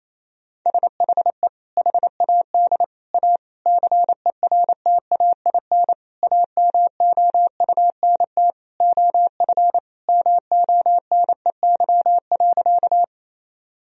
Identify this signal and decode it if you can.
Morse code, heard as a keyed tone, about 28 wpm, 700 Hz. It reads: SHE HAD A CERTAIN AMOUNT OF MONEY.
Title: Sentences from Top 1000 Words - Review Part 4 - 28wpm